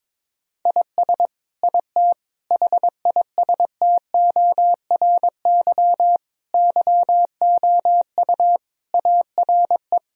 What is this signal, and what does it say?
Morse code, heard as a keyed tone, about 22 wpm, 700 Hz. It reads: IS IT HISTORY YOU ARE
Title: Sentences from Top 1000 Words - Review Part 3 - 22wpm